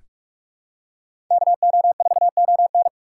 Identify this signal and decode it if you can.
Morse code, heard as a keyed tone, about 44 wpm, 690 Hz. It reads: KO4ON
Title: USA Call Signs - 5 Characters - 6x Character Spacing - 45wpm